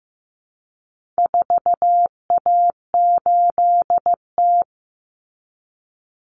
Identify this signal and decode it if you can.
Morse code, heard as a keyed tone, about 15 wpm, 700 Hz. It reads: 4A8T
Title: Common Cut Numbers - 4-Digit 15wpm